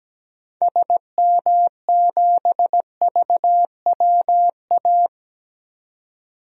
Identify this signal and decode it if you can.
Morse code, heard as a keyed tone, about 17 wpm, 700 Hz. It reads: SM7VWA
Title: DXCC-100 Call Signs - 6 Characters - 17wpm